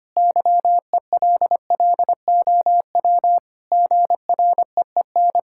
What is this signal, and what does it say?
Morse code, heard as a keyed tone, about 25 wpm, 700 Hz. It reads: YELLOW GREEN
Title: Pangram Challenge v2 - 25wpm